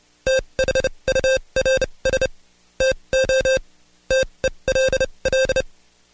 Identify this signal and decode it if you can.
Morse code, heard as a keyed tone, about 29 wpm, spaced as for 23 wpm, 530 Hz. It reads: THURS TO TELL
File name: Code Quick / AudioDownld2 / Seg138-Stepper11 23WPM